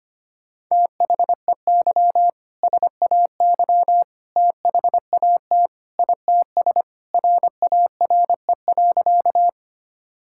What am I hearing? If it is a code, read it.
Morse code, heard as a keyed tone, about 25 wpm, 700 Hz. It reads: THEY SAY THAT ITS RARE.
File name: practice / 25wpm / Sentences with 2-Word Phrases - 25wpm